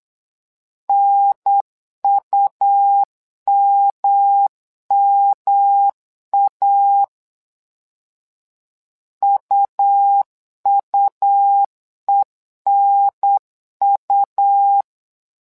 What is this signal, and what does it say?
Morse code, heard as a keyed tone, about 8 wpm, 800 Hz. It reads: NUMMA UUENU
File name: LEZIONI / 3 MWRO / 3 MWRO 8wpm 2606